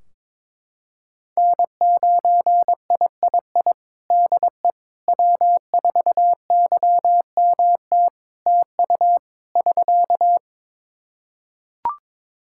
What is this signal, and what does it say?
Morse code, heard as a keyed tone, about 22 wpm, 700 Hz. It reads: N9III DE W4YMT TU <SK>
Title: Practice QSO - Exchange 4 - Ending - 22wpm